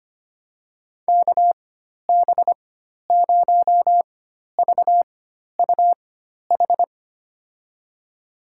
Morse code, heard as a keyed tone, about 25 wpm, spaced as for 12 wpm, 700 Hz. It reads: KB0VUH
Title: USA Call Signs - 6 Characters - 4x Character Spacing - 25wpm